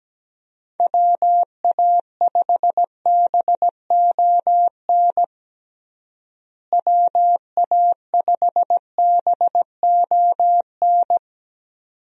Morse code, heard as a keyed tone, about 17 wpm, 700 Hz. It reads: WA5BON WA5BON
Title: USA Call Signs - 6 Characters - Repeated 2x - 17wpm